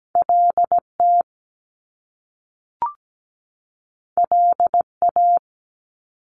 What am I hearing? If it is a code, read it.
Morse code, heard as a keyed tone, about 17 wpm, 700 Hz. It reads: LT LA
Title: Top Two Letter Combinations in Words 17wpm